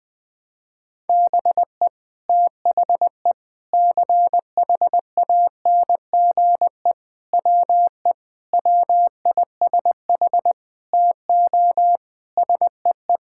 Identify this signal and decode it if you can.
Morse code, heard as a keyed tone, about 20 wpm, 700 Hz. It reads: BE THE CHANGE WE WISH TO SEE
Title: Sets of 8 Words - 20wpm